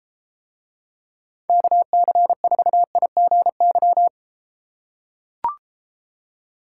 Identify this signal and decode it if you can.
Morse code, heard as a keyed tone, about 33 wpm, 700 Hz. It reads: KC4IGY